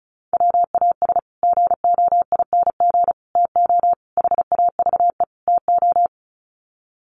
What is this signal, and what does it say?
Morse code, heard as a keyed tone, about 34 wpm, 700 Hz. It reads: WAS GOING TO HAVE TO
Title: Sets of 5 Words Encore 35wpm